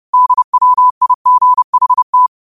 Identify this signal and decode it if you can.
Morse code, heard as a keyed tone, about 30 wpm, 1000 Hz. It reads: DWIGHT